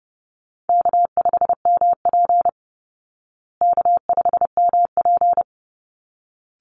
Morse code, heard as a keyed tone, about 30 wpm, 700 Hz. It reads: K5MP K5MP
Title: Call Signs Repeated 3x - 30wpm